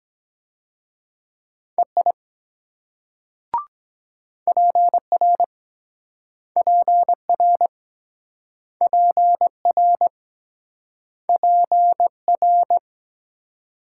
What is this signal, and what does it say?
Morse code, heard as a keyed tone, about 21 wpm, 700 Hz. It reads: EI PR PR PR PR
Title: Speed-Racing Multi-letter Phonemes - 17wpm